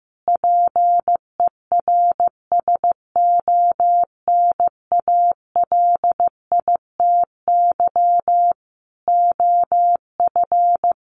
Morse code, heard as a keyed tone, about 14 wpm, 700 Hz. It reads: PERSONALITY OF
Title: Sentences from Top 1400 Words 15wpm